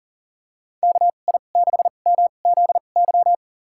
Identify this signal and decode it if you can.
Morse code, heard as a keyed tone, about 40 wpm, spaced as for 29 wpm, 700 Hz. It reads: KI6MZY
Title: USA Call Signs - 6 Characters - 2x Character Spacing - 40wpm